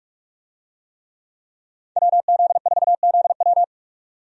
Morse code, heard as a keyed tone, about 47 wpm, 700 Hz. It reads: WZ4ZW
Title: USA Call Signs - 5 Characters - 45wpm